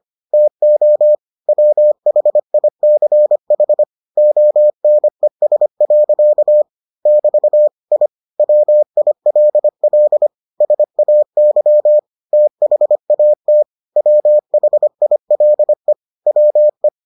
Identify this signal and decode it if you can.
Morse code, heard as a keyed tone, about 25 wpm, 600 Hz. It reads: TO WHICH ONES. <BT> I WILL SAY THAT WHILE WE